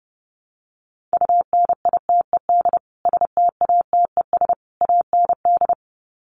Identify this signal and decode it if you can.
Morse code, heard as a keyed tone, about 30 wpm, 700 Hz. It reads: UNITED STATES AND
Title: Sets of 3 Words 30wpm